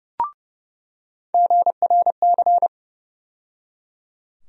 Morse code, heard as a keyed tone, about 30 wpm, 700 Hz. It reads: GRC
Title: USA Call Sign Postfixes - 3 Letters 30wpm